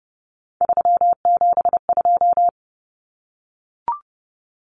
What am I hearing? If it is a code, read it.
Morse code, heard as a keyed tone, about 30 wpm, 700 Hz. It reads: 372
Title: Common Cut Numbers - 3-Digit 30wpm